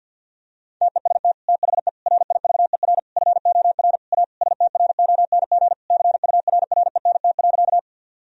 Morse code, heard as a keyed tone, about 50 wpm, 700 Hz. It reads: TEST THE RIVER FOR A STRONG CURRENT.
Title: Sentences from Top 1000 Words - Review Part 4 - 50wpm